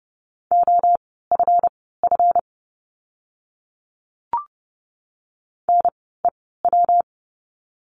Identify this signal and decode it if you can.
Morse code, heard as a keyed tone, about 30 wpm, spaced as for 17 wpm, 700 Hz. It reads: OFF NEW
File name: course / 30:10wpm / 237 Lesson 38f - Letter q - Words 30wpm